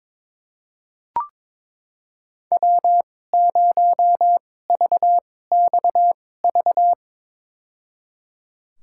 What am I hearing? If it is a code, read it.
Morse code, heard as a keyed tone, about 23 wpm, spaced as for 16 wpm, 700 Hz. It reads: W0VXV
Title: USA Call Signs - 5 Characters - 2x Character Spacing - 22wpm